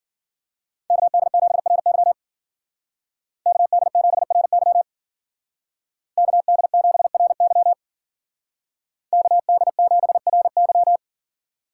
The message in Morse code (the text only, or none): KD7RY KD7RY KD7RY KD7RY